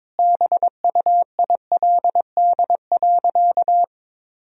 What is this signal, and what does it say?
Morse code, heard as a keyed tone, about 22 wpm, 700 Hz. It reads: BUILD.
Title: Sentences from Top 1000 Words - Review Part 1 - 22wpm